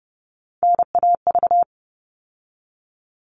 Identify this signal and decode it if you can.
Morse code, heard as a keyed tone, about 30 wpm, 700 Hz. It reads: NAV
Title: All Cut Numbers - 3-Digit 30wpm